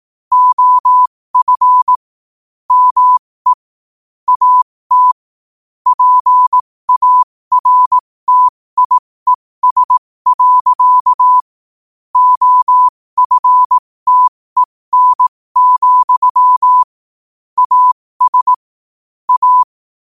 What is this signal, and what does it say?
Morse code, heard as a keyed tone, about 18 wpm, spaced as for 15 wpm, 1000 Hz. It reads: OF ME AT PARTIES. OFTEN, AS A
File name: 15wpm